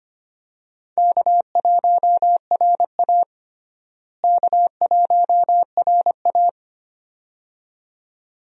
Morse code, heard as a keyed tone, about 25 wpm, 700 Hz. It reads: K1RA K1RA
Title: USA Call Signs - 4 Characters - Repeated 2x - 25wpm